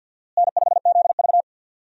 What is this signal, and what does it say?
Morse code, heard as a keyed tone, about 52 wpm, 700 Hz. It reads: N5ZV
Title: USA Call Signs - 4 Characters - Repeated 3x - 50wpm